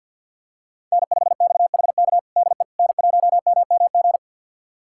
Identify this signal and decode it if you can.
Morse code, heard as a keyed tone, about 52 wpm, 700 Hz. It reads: N5XHK DE N1MMG
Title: USA Callsign DE Callsign - 5 Characters - 50wpm